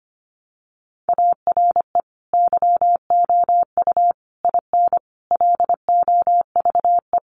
Morse code, heard as a keyed tone, about 25 wpm, 700 Hz. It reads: ARE YOU IN LOVE
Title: Sentences from Top 400 Words 25wpm